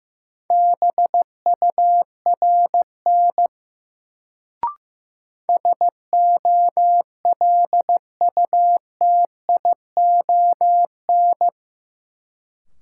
Morse code, heard as a keyed tone, about 15 wpm, 700 Hz. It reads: BURN SOLUTION
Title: Sentences from Top 1300 Words - New Word Drill - 15wpm